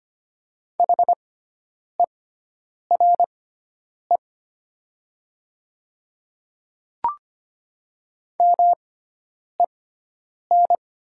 Morse code, heard as a keyed tone, about 25 wpm, spaced as for 9 wpm, 690 Hz. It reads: HERE MEN
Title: 6x Character Spacing - Top 200 Words - 25wpm